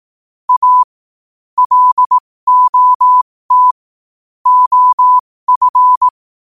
Morse code, heard as a keyed tone, about 18 wpm, spaced as for 15 wpm, 1000 Hz. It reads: A LOT OF
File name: qotdfiles / 15wpm